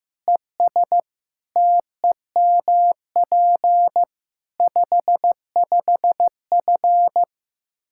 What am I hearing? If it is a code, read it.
Morse code, heard as a keyed tone, about 15 wpm, 700 Hz. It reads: ES TEMP 55F
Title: K1USN SST QSO Practice 15wpm